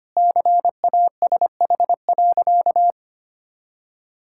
Morse code, heard as a keyed tone, about 25 wpm, 700 Hz. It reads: CASH.
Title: Sentences from Top 1000 Words - Review Part 5 - 25wpm